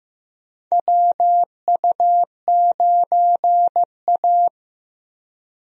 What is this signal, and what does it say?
Morse code, heard as a keyed tone, about 15 wpm, 700 Hz. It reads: WU9A
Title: USA Call Signs - 4 Characters - Repeated 2x - 15wpm